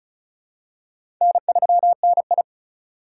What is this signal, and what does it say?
Morse code, heard as a keyed tone, about 36 wpm, 700 Hz. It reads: N3NI
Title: USA Call Signs - 4 Characters - 3x Character Spacing - 35wpm